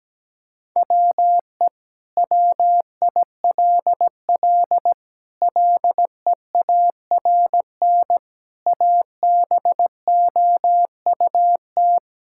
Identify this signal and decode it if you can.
Morse code, heard as a keyed tone, about 17 wpm, 700 Hz. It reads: WE WILL LEARN ABOUT
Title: Sentences from Top 1000 Words - Review Part 5 - 17wpm